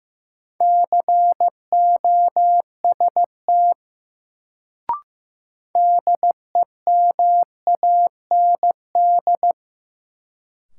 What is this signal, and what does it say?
Morse code, heard as a keyed tone, about 15 wpm, 700 Hz. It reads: COST DEMAND
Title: Sentences from Top 800 Words - New Word Drill - 15wpm